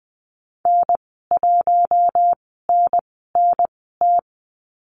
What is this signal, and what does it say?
Morse code, heard as a keyed tone, about 20 wpm, spaced as for 14 wpm, 700 Hz. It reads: N1NNT
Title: 042 Lesson 8g - Number 1 - Call Signs 20wpm